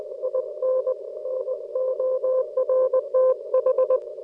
Morse code, heard as a keyed tone, about 21 wpm, 520 Hz. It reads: IN NORTH